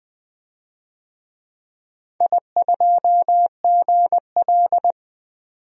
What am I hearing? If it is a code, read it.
Morse code, heard as a keyed tone, about 20 wpm, 700 Hz. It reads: I2GL
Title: DXCC-100 Call Signs - 4 Characters - 20wpm